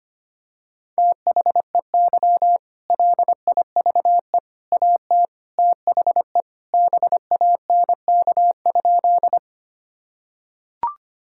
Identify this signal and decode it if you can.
Morse code, heard as a keyed tone, about 25 wpm, 700 Hz. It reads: THEY LIVE AT THE BANK?
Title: Sentences from Top 1000 Words - Review Part 6 - 25wpm